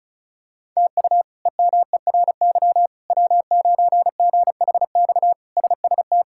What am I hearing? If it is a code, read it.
Morse code, heard as a keyed tone, about 36 wpm, 700 Hz. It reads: TU EMERY W9GHX SST
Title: K1USN SST QSO Practice - 35wpm